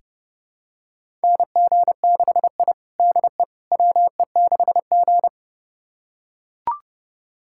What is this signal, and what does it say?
Morse code, heard as a keyed tone, about 30 wpm, 700 Hz. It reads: NG6I DE WE6G
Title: USA Callsign DE Callsign - 4 Characters - 30wpm